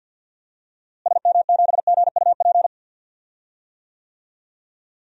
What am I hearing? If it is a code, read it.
Morse code, heard as a keyed tone, about 46 wpm, 700 Hz. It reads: SM7GUP
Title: DXCC-100 Call Signs - 6 Characters - Repeated 2x - 50wpm